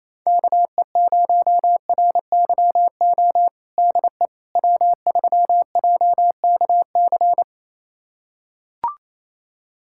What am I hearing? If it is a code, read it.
Morse code, heard as a keyed tone, about 28 wpm, 700 Hz. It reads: KE0RYO DE W3JKC
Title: USA Callsign DE Callsign 28wpm